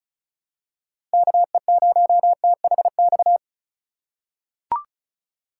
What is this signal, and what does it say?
Morse code, heard as a keyed tone, about 36 wpm, 700 Hz. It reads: KE0THX
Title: USA Call Signs - 2x Character Spacing - 35wpm